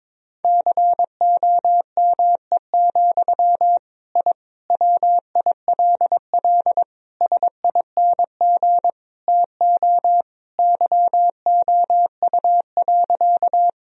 Morse code, heard as a keyed tone, about 21 wpm, 700 Hz. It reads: COME, I WILL SING TO YOU.
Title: Sentences from Top 1000 Words - Review Part 2 - 22wpm